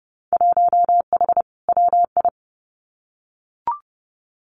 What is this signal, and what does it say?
Morse code, heard as a keyed tone, about 30 wpm, 700 Hz. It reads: 1H WI
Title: Winter Field Day Exchanges 30wpm